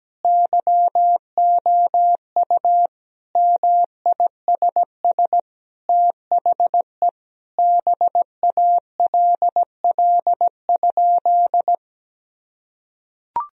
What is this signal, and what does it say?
Morse code, heard as a keyed tone, about 17 wpm, 700 Hz. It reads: YOU MISS THE BALL?
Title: Sentences from Top 1000 Words - Review Part 3 - 17wpm